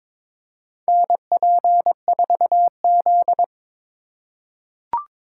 Morse code, heard as a keyed tone, about 22 wpm, 700 Hz. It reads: NP4Z
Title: USA Call Signs - 4 Characters - 2x Character Spacing - 22wpm